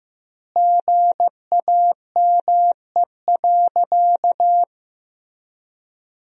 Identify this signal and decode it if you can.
Morse code, heard as a keyed tone, about 15 wpm, 700 Hz. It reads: GAME.
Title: 2x Word-Spacing - Sentences from Top 500 Words - 15wpm